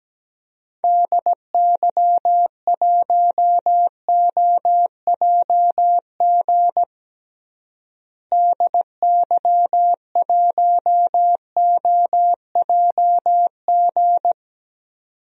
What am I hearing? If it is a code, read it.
Morse code, heard as a keyed tone, about 17 wpm, 700 Hz. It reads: DY1OJG DY1OJG
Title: DXCC-100 Call Signs - 6 Characters - Repeated 2x - 17wpm